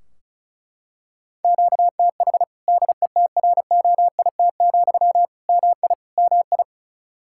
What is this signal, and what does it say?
Morse code, heard as a keyed tone, about 36 wpm, 700 Hz. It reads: QTH DETROIT, MI MI
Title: Practice QSO - Exchange 1 - Beginning - 35wpm